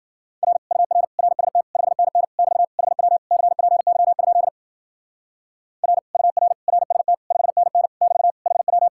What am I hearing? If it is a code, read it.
Morse code, heard as a keyed tone, about 47 wpm, 700 Hz. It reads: R UR RST 5NN <BT> HW CPY? R UR RST 5NN <BT> HW
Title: Speed-Racing Common QSO Phrases Encore - 40wpm